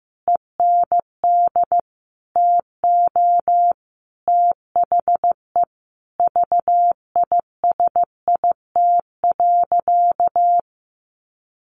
Morse code, heard as a keyed tone, about 15 wpm, 700 Hz. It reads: END TO THE VISIT.
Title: Sentences from Top 1200 Words 15wpm